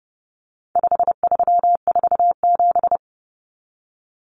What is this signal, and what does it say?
Morse code, heard as a keyed tone, about 30 wpm, 700 Hz. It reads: 5347
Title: Common Cut Numbers - 4-Digit 30wpm